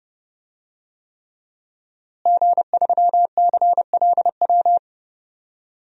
Morse code, heard as a keyed tone, about 30 wpm, 700 Hz. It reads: G3CLW
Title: DXCC-100 Call Signs - 5 Characters - Repeated 3x - 30wpm